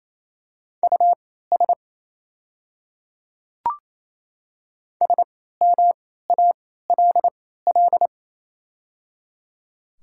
Morse code, heard as a keyed tone, about 28 wpm, spaced as for 16 wpm, 700 Hz. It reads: US SMALL
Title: 3x Character Spacing - Top 200 Words - 28wpm